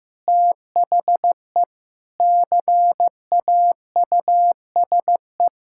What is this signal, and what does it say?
Morse code, heard as a keyed tone, about 15 wpm, 700 Hz. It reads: THE CAUSE
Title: Sentences from Top 1600 Words 15wpm